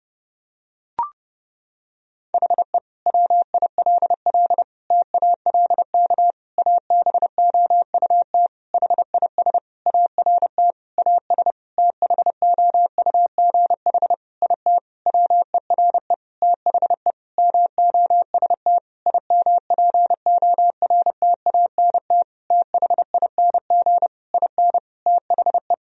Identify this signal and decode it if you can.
Morse code, heard as a keyed tone, about 32 wpm, 700 Hz. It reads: HE WILL TALK ABOUT HIS ART AS THOUGH IT WERE THE MOST IMPORTANT THING IN THE
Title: Sentences from Top 1000 Words - Review Part 3 - 30wpm